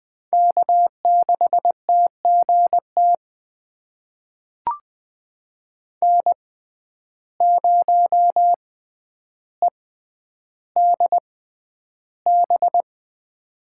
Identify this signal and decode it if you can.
Morse code, heard as a keyed tone, about 20 wpm, spaced as for 10 wpm, 700 Hz. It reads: K6TGT N0EDB
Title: USA Call Signs - 5 Characters - 6x Character Spacing - 20wpm